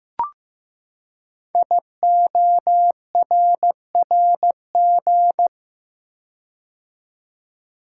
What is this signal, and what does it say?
Morse code, heard as a keyed tone, about 15 wpm, 700 Hz. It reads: IORRG
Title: Code Groups - 5 Letters - 15wpm